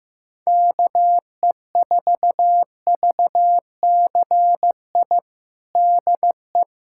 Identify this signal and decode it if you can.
Morse code, heard as a keyed tone, about 15 wpm, 700 Hz. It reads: KE4VCI DE
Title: USA Callsign DE Callsign - 6 Characters - 15wpm